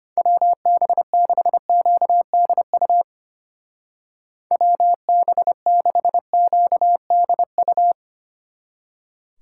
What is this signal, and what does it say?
Morse code, heard as a keyed tone, about 32 wpm, 700 Hz. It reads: WB6QDU WB6QDU